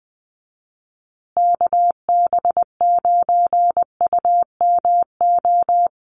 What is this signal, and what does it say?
Morse code, heard as a keyed tone, about 20 wpm, 700 Hz. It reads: KB9UMO
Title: USA Call Signs Repeated 2x 20wpm